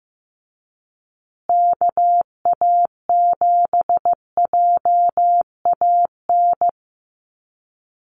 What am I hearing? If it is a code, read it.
Morse code, heard as a keyed tone, about 15 wpm, 700 Hz. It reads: KA7JAN
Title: USA Call Signs Repeated 2x 15wpm